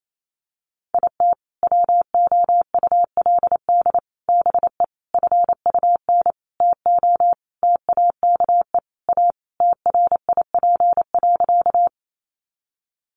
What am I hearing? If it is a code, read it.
Morse code, heard as a keyed tone, about 28 wpm, 700 Hz. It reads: IT WOULD BE FUN TO TAKE A TRIP.
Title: Sentences from Top 1300 Words 28wpm